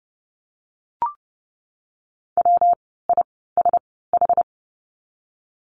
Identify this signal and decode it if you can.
Morse code, heard as a keyed tone, about 31 wpm, spaced as for 17 wpm, 700 Hz. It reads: WISH
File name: course / 30:10wpm / 115 Lesson 19f - Letter w - Words 30wpm